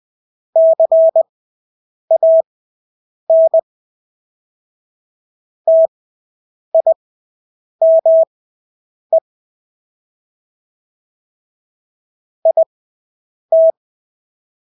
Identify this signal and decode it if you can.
Morse code, heard as a keyed tone, about 20 wpm, spaced as for 8 wpm, 650 Hz. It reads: CAN TIME IT
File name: Engelske ord 20/8 wpm